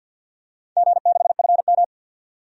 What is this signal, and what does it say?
Morse code, heard as a keyed tone, about 51 wpm, 700 Hz. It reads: G6FM
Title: DXCC-100 Call Signs - 4 Characters - 50wpm